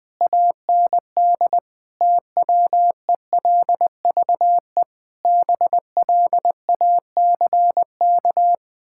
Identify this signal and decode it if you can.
Morse code, heard as a keyed tone, about 20 wpm, 700 Hz. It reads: AND TWELVE BLACK
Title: Pangram Challenge v2 - 20wpm